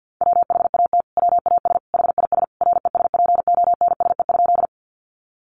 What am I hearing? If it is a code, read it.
Morse code, heard as a keyed tone, about 50 wpm, 700 Hz. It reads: WHAT WAS HIS RESPONSE?
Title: Sentences from Top 1400 Words 50wpm